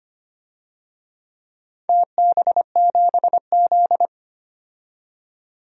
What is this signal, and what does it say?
Morse code, heard as a keyed tone, about 25 wpm, 700 Hz. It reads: TB7Z
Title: DXCC-100 Call Signs - 4 Characters - 25wpm